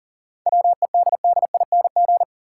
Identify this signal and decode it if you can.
Morse code, heard as a keyed tone, about 40 wpm, 700 Hz. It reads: WEDDING
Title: Seven Letter Words - 40wpm